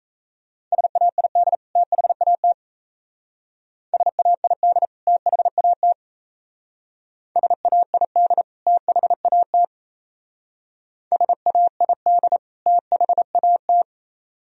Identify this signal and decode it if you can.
Morse code, heard as a keyed tone, about 33 wpm, 700 Hz. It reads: SAID THAT SAID THAT SAID THAT SAID THAT
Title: Speed-Racing - Sets of 2 Words 28wpm